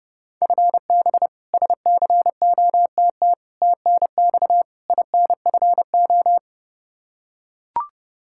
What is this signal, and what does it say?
Morse code, heard as a keyed tone, about 30 wpm, 700 Hz. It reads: FB SCOTT TNX INFO
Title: K1USN SST QSO Practice 30wpm